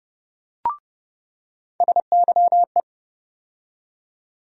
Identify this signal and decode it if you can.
Morse code, heard as a keyed tone, about 30 wpm, 700 Hz. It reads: SYE
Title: USA Call Sign Postfixes - 3 Letters 30wpm